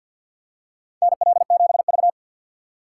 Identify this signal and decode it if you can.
Morse code, heard as a keyed tone, about 53 wpm, 700 Hz. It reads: NL7V
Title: K1USN SST QSO Practice - 50wpm